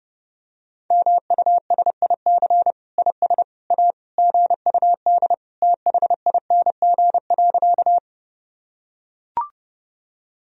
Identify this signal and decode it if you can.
Morse code, heard as a keyed tone, about 30 wpm, 700 Hz. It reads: MUSIC IS A GUD THING.
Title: Sentences from Top 300 Words and Abbreviations 30wpm